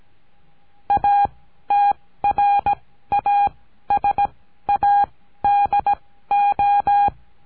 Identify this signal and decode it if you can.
Morse code, heard as a keyed tone, about 17 wpm, spaced as for 13 wpm, 810 Hz. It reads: ATRASADO
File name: CURSO DE TELEGRAFIA